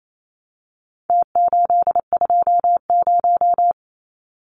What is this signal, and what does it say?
Morse code, heard as a keyed tone, about 28 wpm, 700 Hz. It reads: T820
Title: Numbers - Contest Style - 4-Digit 28wpm